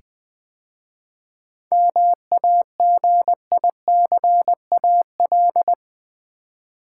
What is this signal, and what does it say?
Morse code, heard as a keyed tone, about 20 wpm, 700 Hz. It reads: MAGICAL
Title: 7 Letter Words - First 3 Letters Given - 20wpm